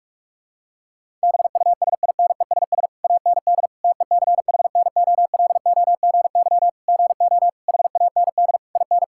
Morse code, heard as a keyed tone, about 46 wpm, 700 Hz. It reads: BUSINESS AND TECHNOLOGY GO HAND IN
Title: Sentences from Top 1000 Words - Review Part 5 - 45wpm